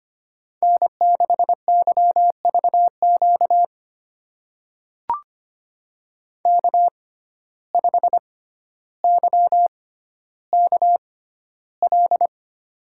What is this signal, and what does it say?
Morse code, heard as a keyed tone, about 25 wpm, spaced as for 13 wpm, 700 Hz. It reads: N6YVQ K5YKL
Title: USA Call Signs - 5 Characters - 6x Character Spacing - 25wpm